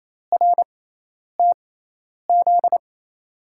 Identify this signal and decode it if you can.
Morse code, heard as a keyed tone, about 28 wpm, spaced as for 10 wpm, 700 Hz. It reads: RTZ
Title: USA Call Signs - 5 Characters - 6x Character Spacing - 28wpm